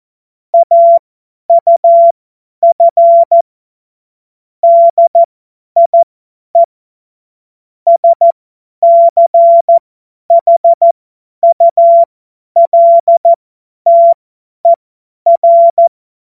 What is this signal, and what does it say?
Morse code, heard as a keyed tone, about 14 wpm, spaced as for 10 wpm, 680 Hz. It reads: AUF DIE SCHULTER
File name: Morsekurs / Witze / Witz 08. Februar 2022 50 F70